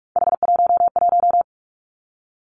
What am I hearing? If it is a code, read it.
Morse code, heard as a keyed tone, about 46 wpm, 700 Hz. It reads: H11
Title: Three Challenging Letter-Number Combinations 45wpm